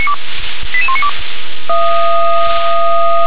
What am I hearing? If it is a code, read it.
Five-tone selective call (ZVEI), heard as quick runs of five tones, tones of about 80 ms. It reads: 88192
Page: ZVEI